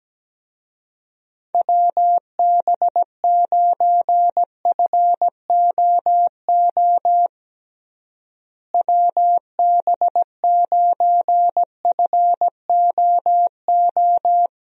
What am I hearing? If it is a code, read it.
Morse code, heard as a keyed tone, about 17 wpm, 700 Hz. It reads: WB9FOO WB9FOO
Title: USA Call Signs - 6 Characters - Repeated 3x - 17wpm